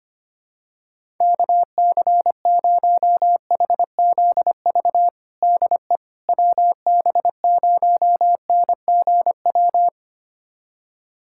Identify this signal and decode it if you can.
Morse code, heard as a keyed tone, about 25 wpm, 700 Hz. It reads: KC0HZV DE WB0NGW
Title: USA Callsign DE Callsign - 6 Characters - 25wpm